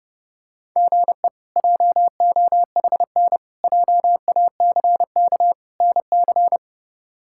Morse code, heard as a keyed tone, about 32 wpm, 700 Hz. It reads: GE JOHN JACK NC
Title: K1USN SST QSO Practice - 30wpm